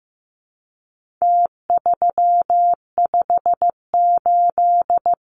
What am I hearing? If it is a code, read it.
Morse code, heard as a keyed tone, about 15 wpm, 700 Hz. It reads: T358
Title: Numbers - Contest Style - 4-Digit 15wpm